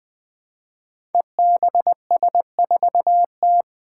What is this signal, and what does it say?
Morse code, heard as a keyed tone, about 20 wpm, 700 Hz. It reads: EBS4T
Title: Five Challenging Letter-Number Combinations 20wpm